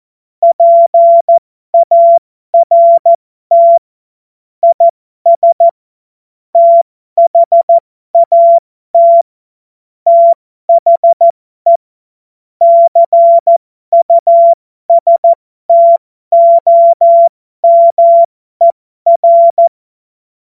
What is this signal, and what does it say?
Morse code, heard as a keyed tone, about 14 wpm, spaced as for 12 wpm, 680 Hz. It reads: PART IS THAT THE CUSTOMER
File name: Morsekurs / Witze